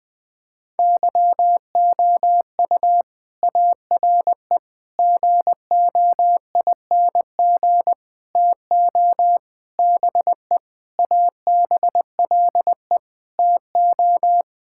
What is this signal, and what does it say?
Morse code, heard as a keyed tone, about 20 wpm, 700 Hz. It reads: YOU ARE GOING TO BE ABLE TO
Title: Sentences with 4-Word Phrases - 20wpm